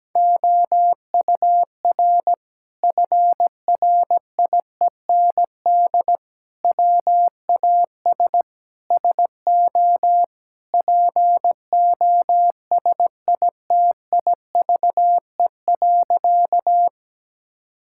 Morse code, heard as a keyed tone, about 17 wpm, 700 Hz. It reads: OUR FRIEND WAS SO POSITIVE.
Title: Sentences from Top 1000 Words - Review Part 5 - 17wpm